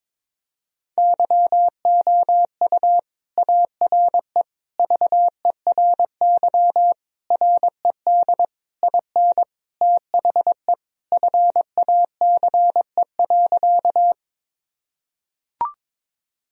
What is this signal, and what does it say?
Morse code, heard as a keyed tone, about 22 wpm, 700 Hz. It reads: YOU ARE VERY RED IN THE FACE.
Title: Sentences from Top 1000 Words - Review Part 2 - 22wpm